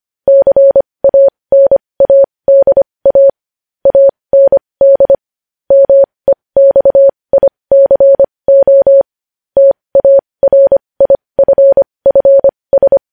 Morse code, heard as a keyed tone, about 25 wpm, spaced as for 20 wpm, 550 Hz. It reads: CANADA AND MEXICO TARIFFS
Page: news-headlines-in-morse-code-at-20-wpm